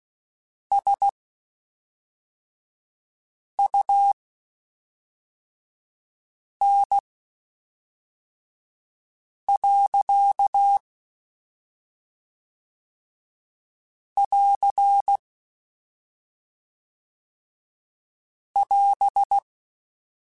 Morse code, heard as a keyed tone, about 16 wpm, spaced as for 3 wpm, 800 Hz. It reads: SUN. <AR> <AS>
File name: Spaced Characters / Spaced Characters-mp3s / Weather5 5 sp @5